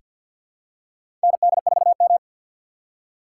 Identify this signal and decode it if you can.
Morse code, heard as a keyed tone, about 46 wpm, 700 Hz. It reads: ND4M
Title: USA Call Signs - 4 Characters - 3x Character Spacing - 50wpm